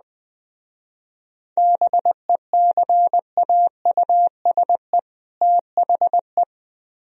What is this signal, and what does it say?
Morse code, heard as a keyed tone, about 20 wpm, 700 Hz. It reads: BECAUSE THE